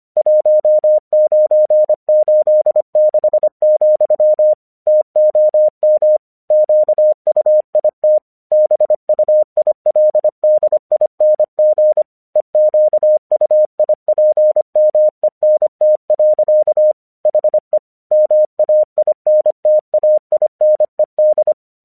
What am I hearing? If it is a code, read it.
Morse code, heard as a keyed tone, about 25 wpm, 610 Hz. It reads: 1986, TOM QUIT BUILDING EQUIPMENT. HE MAINTAINED
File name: 1 English 25wpm / AQC13